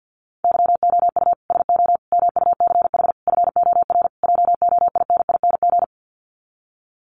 Morse code, heard as a keyed tone, about 50 wpm, 700 Hz. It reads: YOU SO MUCH FOR JOINING
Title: Sets of 5 Words Encore2 50wpm